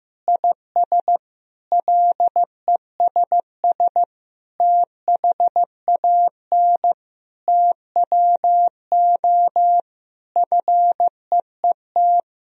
Morse code, heard as a keyed tone, about 15 wpm, 700 Hz. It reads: IS LESS THAN TWO FEET
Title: Sentences from Top 1000 Words - Review Part 2 - 15wpm